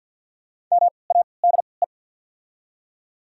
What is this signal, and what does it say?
Morse code, heard as a keyed tone, about 49 wpm, 690 Hz. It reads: M A D E
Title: 3x Character Spacing - Top 200 Words - 50wpm